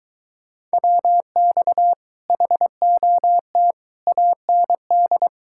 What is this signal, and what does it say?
Morse code, heard as a keyed tone, about 23 wpm, 700 Hz. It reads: WX HOT AND